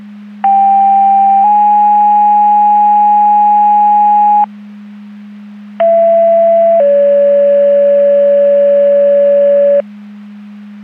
Two-tone page, heard as a one-second tone, then a three-second tone, about 790 then 830 Hz, about 670 then 570 Hz. It is heard from 0.4 to 9.8 s.
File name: Portage County / 00721 02-47 Portage County 154.3850 Fire EMS Paging